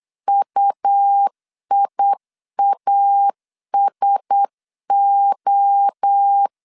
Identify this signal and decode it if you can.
Morse code, heard as a keyed tone, about 8 wpm, 800 Hz. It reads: UIASO
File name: CW / DET / 1 6 1500C S C 8wpm 5219